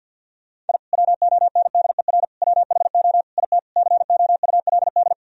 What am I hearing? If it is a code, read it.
Morse code, heard as a keyed tone, about 50 wpm, 700 Hz. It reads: I WONDER WHO IT COULD
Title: Sentences from Top 1000 Words - Review Part 3 - 50wpm